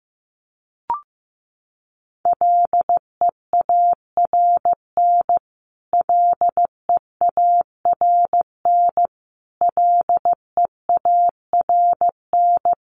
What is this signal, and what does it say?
Morse code, heard as a keyed tone, about 15 wpm, 700 Hz. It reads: LEARN LEARN LEARN
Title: Top 500 Words Repeated 15wpm